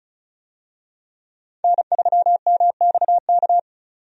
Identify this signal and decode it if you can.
Morse code, heard as a keyed tone, about 34 wpm, 700 Hz. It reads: N3MXK
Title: USA Call Signs - 5 Characters - 35wpm